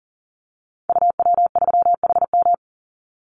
Morse code, heard as a keyed tone, about 37 wpm, 700 Hz. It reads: UW3HM
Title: Call Signs Repeated 3x - 40wpm